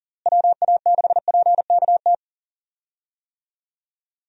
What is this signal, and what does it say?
Morse code, heard as a keyed tone, about 38 wpm, 700 Hz. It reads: WA6PKT
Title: USA Call Signs - 6 Characters - Repeated 2x - 40wpm